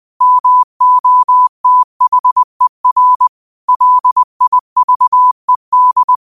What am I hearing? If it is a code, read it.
Morse code, heard as a keyed tone, about 20 wpm, 1000 Hz. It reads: MOTHER LIVED
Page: New quotes every day in morse code at 20 Words per minute.